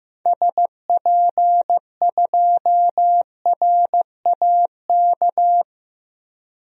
Morse code, heard as a keyed tone, about 15 wpm, 700 Hz. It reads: SP2RAK
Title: DXCC-100 Call Signs - 6 Characters - 15wpm